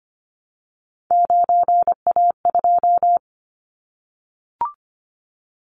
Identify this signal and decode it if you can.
Morse code, heard as a keyed tone, about 25 wpm, 700 Hz. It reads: 9A2
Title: Callsign Prefixes for Top 100 DXCC - 25wpm